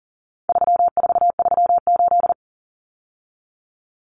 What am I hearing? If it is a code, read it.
Morse code, heard as a keyed tone, about 40 wpm, 700 Hz. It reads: 3438
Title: Numbers - 4-Digit 40wpm